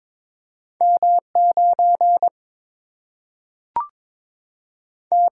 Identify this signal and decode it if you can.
Morse code, heard as a keyed tone, about 22 wpm, 700 Hz. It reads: M 9 T
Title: Single-Letter Single-Number 22wpm